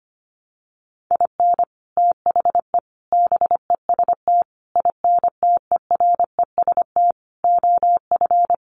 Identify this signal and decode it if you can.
Morse code, heard as a keyed tone, about 25 wpm, 700 Hz. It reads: IN THE BEST INTEREST OF
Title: Sets of 5 Words Encore 25wpm